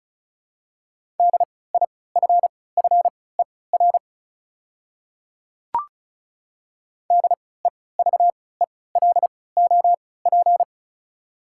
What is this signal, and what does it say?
Morse code, heard as a keyed tone, about 36 wpm, spaced as for 20 wpm, 690 Hz. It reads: DIFFER DEVELOP
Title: 3x Character Spacing - Top 500 Words - 35wpm